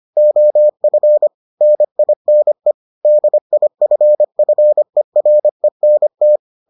Morse code, heard as a keyed tone, about 25 wpm, 600 Hz. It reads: OF NINE DIFFERENT